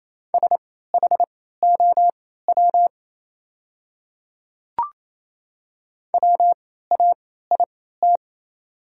Morse code, heard as a keyed tone, about 27 wpm, spaced as for 16 wpm, 690 Hz. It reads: SHOW WAIT
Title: 3x Character Spacing - Top 500 Words - 28wpm